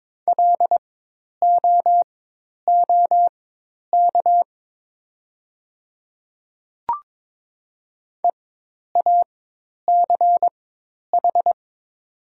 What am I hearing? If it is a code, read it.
Morse code, heard as a keyed tone, about 22 wpm, spaced as for 10 wpm, 700 Hz. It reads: LOOK EACH
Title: 4x Character Spacing - Top 100 Words - 22wpm